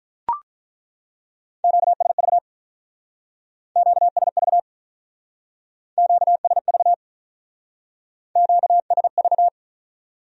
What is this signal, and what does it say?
Morse code, heard as a keyed tone, about 43 wpm, 700 Hz. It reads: QSV QSV QSV QSV
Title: Speed-Racing Q-Codes - 35wpm